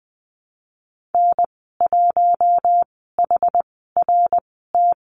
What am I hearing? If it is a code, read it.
Morse code, heard as a keyed tone, about 20 wpm, spaced as for 14 wpm, 700 Hz. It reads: N1HRT
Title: 082 Lesson 14g - Number 2 - Call Signs 20wpm